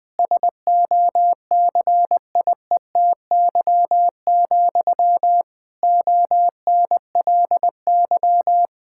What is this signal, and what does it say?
Morse code, heard as a keyed tone, about 20 wpm, 700 Hz. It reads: SOCIETY, ONLY
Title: Sentences from Top 1000 Words - Review Part 6 - 20wpm